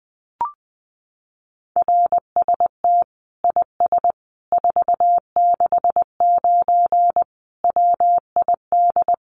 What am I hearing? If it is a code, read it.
Morse code, heard as a keyed tone, about 20 wpm, 700 Hz. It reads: RST IS 469 WID
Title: Common QSO Phrases 20wpm